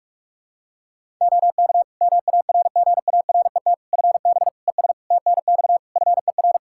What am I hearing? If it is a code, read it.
Morse code, heard as a keyed tone, about 46 wpm, 700 Hz. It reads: OK MARGARET FB ES TNX FER
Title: Practice QSO - Exchange 2 - Info - 45wpm